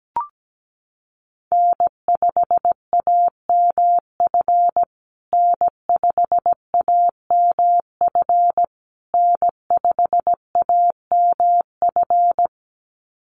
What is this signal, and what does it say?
Morse code, heard as a keyed tone, about 17 wpm, 700 Hz. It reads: N5AMF N5AMF N5AMF
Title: USA Call Signs Repeated 3x 17wpm